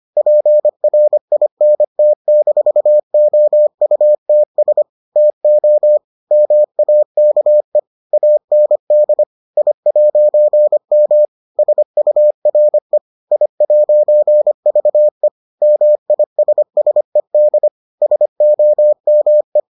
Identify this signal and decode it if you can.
Morse code, heard as a keyed tone, about 25 wpm, 600 Hz. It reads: PRINT-OUTS TO MAKE AND I'M SURE I'VE MISSED SOME